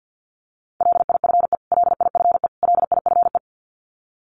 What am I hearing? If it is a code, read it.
Morse code, heard as a keyed tone, about 50 wpm, 700 Hz. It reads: LIFE LIFE LIFE
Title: Top 500 Words Repeated 50wpm